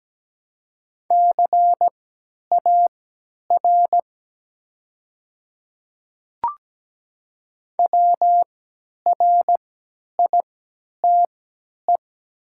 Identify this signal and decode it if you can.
Morse code, heard as a keyed tone, about 17 wpm, spaced as for 10 wpm, 700 Hz. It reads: CAR WRITE
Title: 3x Character Spacing - Top 500 Words - 17wpm